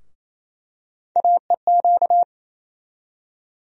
Morse code, heard as a keyed tone, about 28 wpm, 700 Hz. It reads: AEQ
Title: USA Call Sign Postfixes - 3 Letters 28wpm